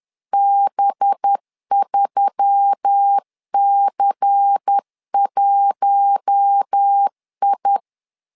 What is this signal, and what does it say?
Morse code, heard as a keyed tone, about 10 wpm, 800 Hz. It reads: B3C1I
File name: LEZIONI / 5 LBVC / 5 LBVC 10wpm 2349